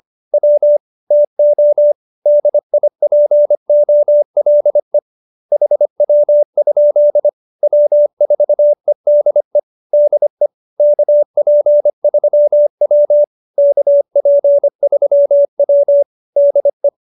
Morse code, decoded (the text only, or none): W TO DIPOLE HW? W4EDE DE KP3W KP3W DE